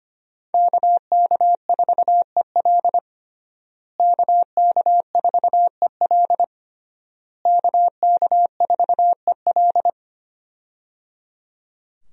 Morse code, heard as a keyed tone, about 25 wpm, 700 Hz. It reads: KK4EL KK4EL KK4EL
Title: USA Call Signs - 5 Characters - Repeated 3x - 25wpm